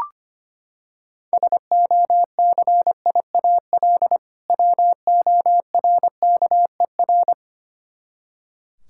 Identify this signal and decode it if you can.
Morse code, heard as a keyed tone, about 25 wpm, 700 Hz. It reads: SOCIAL WORKER
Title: Practice QSO - Exchange 3 - More Info - 25wpm